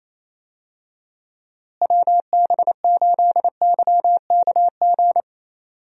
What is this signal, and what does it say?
Morse code, heard as a keyed tone, about 28 wpm, 700 Hz. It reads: WB8YKG
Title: USA Call Signs - 6 Characters - Repeated 2x - 28wpm